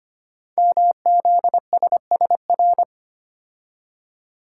Morse code, heard as a keyed tone, about 25 wpm, 700 Hz. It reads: MZSSR
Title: Code Groups - 5 Letters - 25wpm